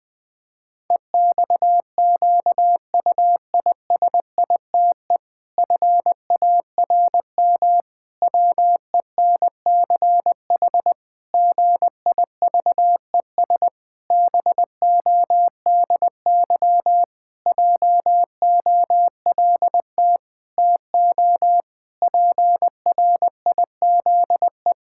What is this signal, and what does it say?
Morse code, heard as a keyed tone, about 20 wpm, 700 Hz. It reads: EXQUISITE FARM WENCH GIVES BODY JOLT TO PRIZE
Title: Pangram Challenge v2 - 20wpm